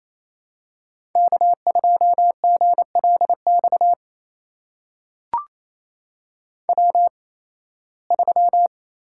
Morse code, heard as a keyed tone, about 28 wpm, spaced as for 18 wpm, 700 Hz. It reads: K2GLX W3
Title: USA Call Signs - 5 Characters - 8x Character Spacing - 28wpm